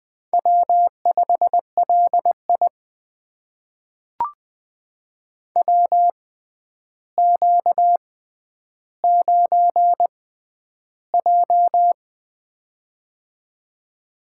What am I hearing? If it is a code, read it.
Morse code, heard as a keyed tone, about 20 wpm, spaced as for 10 wpm, 700 Hz. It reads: W5LI WQ9J
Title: USA Call Signs - 6x Character Spacing - 20wpm